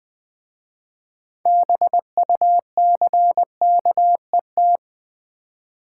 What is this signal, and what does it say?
Morse code, heard as a keyed tone, about 20 wpm, 700 Hz. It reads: BUCKET
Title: Easy Words with Hints - 20wpm